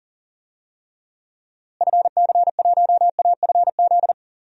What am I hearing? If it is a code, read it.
Morse code, heard as a keyed tone, about 39 wpm, 700 Hz. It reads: FC1AFZ
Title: DXCC-100 Call Signs - 6 Characters - Repeated 2x - 40wpm